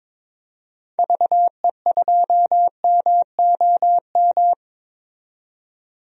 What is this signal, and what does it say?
Morse code, heard as a keyed tone, about 22 wpm, 700 Hz. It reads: VE2MOM
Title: DXCC-100 Call Signs - 6 Characters - 22wpm